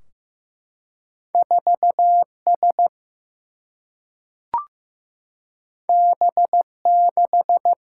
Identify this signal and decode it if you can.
Morse code, heard as a keyed tone, about 16 wpm, 700 Hz. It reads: 4 S B 6
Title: Single-Letter Single-Number 15wpm